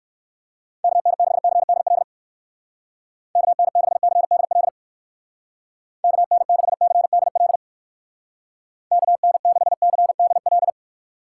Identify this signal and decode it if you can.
Morse code, heard as a keyed tone, about 56 wpm, 700 Hz. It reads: KN6CDL KN6CDL KN6CDL KN6CDL
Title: Speed Racing - USA 2x3 Call Signs 45wpm